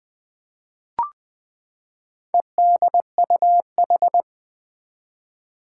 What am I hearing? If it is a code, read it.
Morse code, heard as a keyed tone, about 20 wpm, 700 Hz. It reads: EDUH
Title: Four Challenging Letter-Number Combinations 20wpm